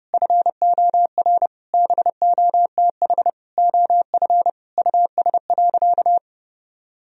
Morse code, heard as a keyed tone, about 30 wpm, 700 Hz. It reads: FOR BOTH OF US.
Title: Sentences from Top 1000 Words - Review Part 1 - 30wpm